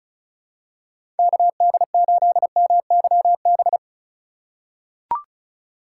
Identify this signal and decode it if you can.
Morse code, heard as a keyed tone, about 36 wpm, 700 Hz. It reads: KD8MYB
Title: USA Call Signs - 6 Characters - 3x Character Spacing - 35wpm